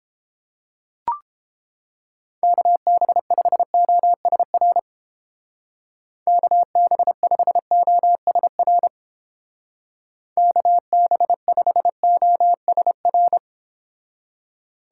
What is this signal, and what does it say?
Morse code, heard as a keyed tone, about 29 wpm, 700 Hz. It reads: KB5OSR KB5OSR KB5OSR